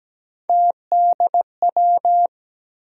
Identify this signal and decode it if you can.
Morse code, heard as a keyed tone, about 17 wpm, 700 Hz. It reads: TDW
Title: USA Call Sign Postfixes - 3 Letters 17wpm